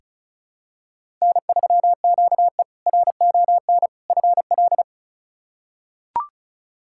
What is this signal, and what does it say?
Morse code, heard as a keyed tone, about 34 wpm, 700 Hz. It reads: N3QE RON FL
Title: K1USN SST QSO Practice - 35wpm